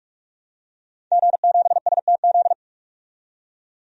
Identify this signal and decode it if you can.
Morse code, heard as a keyed tone, about 42 wpm, 700 Hz. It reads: G7STZ
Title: DXCC-100 Call Signs - 5 Characters - 45wpm